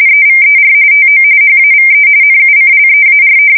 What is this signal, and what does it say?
RTTY, about 45.45 baud. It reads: EVER TO HAVE NUCLEAR
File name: fsk170 / USS-New-Jersey